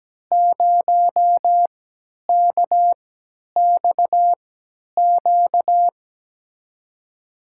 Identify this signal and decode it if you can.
Morse code, heard as a keyed tone, about 17 wpm, spaced as for 10 wpm, 700 Hz. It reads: 0KXQ
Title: USA Call Signs - 6 Characters - 3x Character Spacing - 17wpm